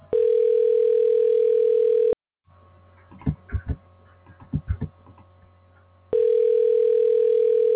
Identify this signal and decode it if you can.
Ringback tone (North America), 440 + 480 Hz.